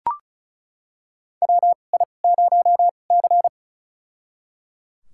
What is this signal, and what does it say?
Morse code, heard as a keyed tone, about 36 wpm, spaced as for 26 wpm, 700 Hz. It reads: WI0C
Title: USA Call Signs - 4 Characters - 2x Character Spacing - 35wpm